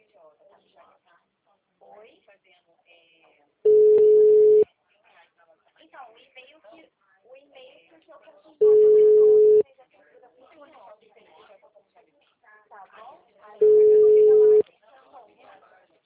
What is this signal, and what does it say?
Ringback tone (Europe, 425 Hz), with speech.